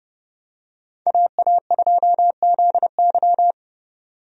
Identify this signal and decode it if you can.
Morse code, heard as a keyed tone, about 30 wpm, 700 Hz. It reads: AA2ZY
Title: USA Call Signs - 5 Characters - 30wpm